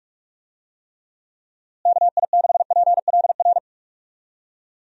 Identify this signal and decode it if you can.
Morse code, heard as a keyed tone, about 46 wpm, 700 Hz. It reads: KI6PLR
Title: USA Call Signs - 6 Characters - Repeated 2x - 45wpm